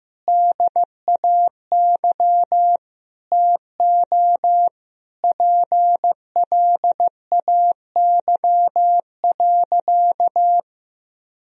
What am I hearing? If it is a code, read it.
Morse code, heard as a keyed tone, about 15 wpm, 700 Hz. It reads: DAY TO PLAY.
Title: Sentences from Top 1000 Words - Review Part 1 - 15wpm